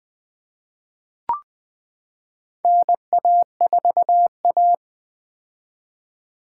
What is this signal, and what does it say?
Morse code, heard as a keyed tone, about 20 wpm, 700 Hz. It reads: NA4A
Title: 2023 POTA Activator Call Signs - 20wpm